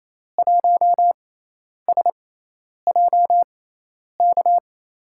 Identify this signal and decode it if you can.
Morse code, heard as a keyed tone, about 28 wpm, spaced as for 10 wpm, 700 Hz. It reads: 1SJK
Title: USA Call Signs - 6 Characters - 6x Character Spacing - 28wpm